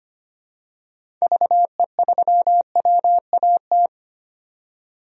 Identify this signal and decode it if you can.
Morse code, heard as a keyed tone, about 25 wpm, 700 Hz. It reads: VE3WAT
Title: DXCC-100 Call Signs - 6 Characters - 25wpm